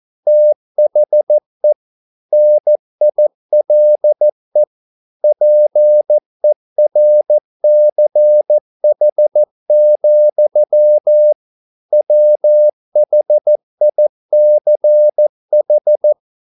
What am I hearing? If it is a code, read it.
Morse code, heard as a keyed tone, about 14 wpm, 600 Hz. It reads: THE NILE PERCH, WHICH